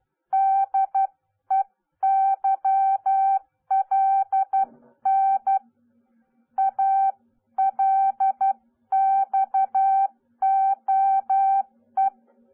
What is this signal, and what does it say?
Morse code, heard as a keyed tone, about 12 wpm, 780 Hz. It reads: DEYLN ALXOE